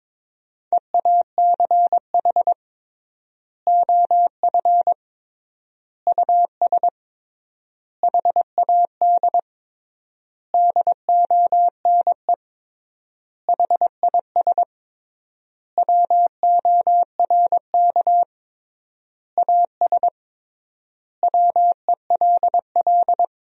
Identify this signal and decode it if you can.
Morse code, heard as a keyed tone, about 22 wpm, 700 Hz. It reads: EACH OF US HAD DONE HIS WORK AS WELL
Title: 3x Word-Spacing - Sentences from Top 500 Words - 22wpm